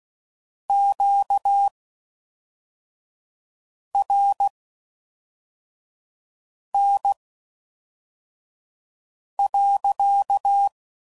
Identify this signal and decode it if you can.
Morse code, heard as a keyed tone, about 16 wpm, spaced as for 4 wpm, 800 Hz. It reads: QRN.